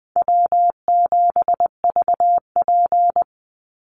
Morse code, heard as a keyed tone, about 20 wpm, 700 Hz. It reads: W7VP
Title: Call Signs Repeated 2x - 20wpm